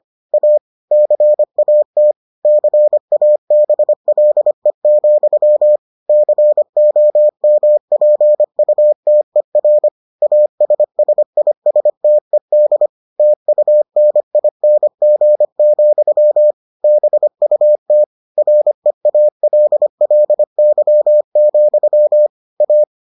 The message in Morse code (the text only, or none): A CAT CABLE, COMPUTER ASSISTED TUNING, BUT REALLY, A